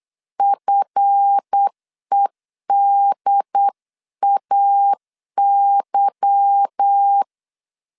Morse code, heard as a keyed tone, about 8 wpm, 800 Hz. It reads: FEDAY
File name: CW / DET / 1 9 1500C S B 8wpm 5624